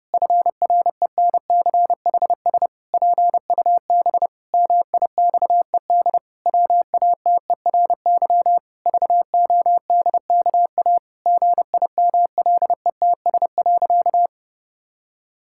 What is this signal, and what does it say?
Morse code, heard as a keyed tone, about 30 wpm, 700 Hz. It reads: FRENCHS PUB MIXED WATERY VODKA GIMLETS.
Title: Pangram Challenge v2 - 30wpm